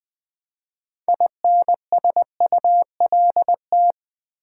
Morse code, heard as a keyed tone, about 21 wpm, 700 Hz. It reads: INSULT
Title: 6 Letter Words - First 3 Letters Given - 20wpm